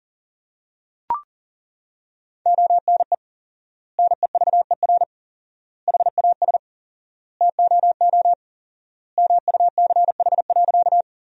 Morse code, heard as a keyed tone, about 40 wpm, 700 Hz. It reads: ONE NEVER HAS TOO MUCH.
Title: 4x Word-Spacing - Sentences from Top 300 Words - 40wpm